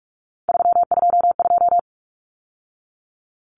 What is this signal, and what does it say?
Morse code, heard as a keyed tone, about 44 wpm, 700 Hz. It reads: 322
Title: Numbers - 3-Digit 45wpm